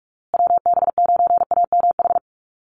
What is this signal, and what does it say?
Morse code, heard as a keyed tone, about 44 wpm, 700 Hz. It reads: WB9AMH
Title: USA Call Signs Repeated 3x 45wpm